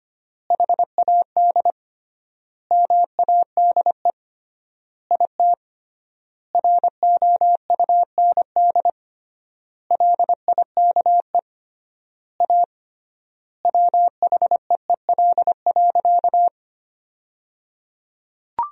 Morse code, heard as a keyed tone, about 25 wpm, 700 Hz. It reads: HAD MADE IT ROUND LIKE A WHEEL.
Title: 3x Word-Spacing - Sentences from Top 500 Words - 25wpm